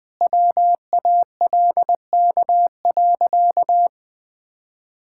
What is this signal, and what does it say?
Morse code, heard as a keyed tone, about 21 wpm, 700 Hz. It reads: WALK.
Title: Sentences with 3-Word Phrases - Spoken Beginning - 20wpm